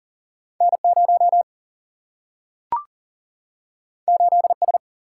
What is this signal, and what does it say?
Morse code, heard as a keyed tone, about 40 wpm, 700 Hz. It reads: N 0 8 S
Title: Single-Letter Single-Number 40wpm